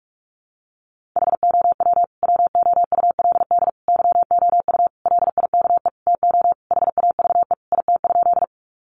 Morse code, heard as a keyed tone, about 44 wpm, 700 Hz. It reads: HOW WOULD YOU LIKE TO HAVE IT?
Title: Sentences from Top 100 Words 45wpm